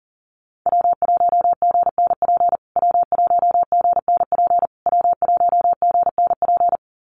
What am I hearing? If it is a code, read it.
Morse code, heard as a keyed tone, about 37 wpm, 700 Hz. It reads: W1GNP W1GNP W1GNP
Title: USA Call Signs Repeated 3x 40wpm